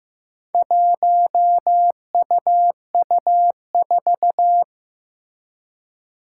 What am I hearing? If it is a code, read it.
Morse code, heard as a keyed tone, about 15 wpm, 700 Hz. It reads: 1UU4
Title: Four Challenging Letter-Number Combinations 15wpm